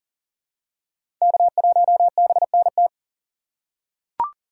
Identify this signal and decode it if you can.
Morse code, heard as a keyed tone, about 40 wpm, 700 Hz. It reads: K1BNT
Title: USA Call Signs - 5 Characters - 6x Character Spacing - 40wpm